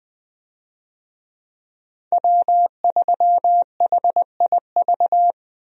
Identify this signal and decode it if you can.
Morse code, heard as a keyed tone, about 20 wpm, 700 Hz. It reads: W3HIV
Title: USA Call Signs - 5 Characters - 20wpm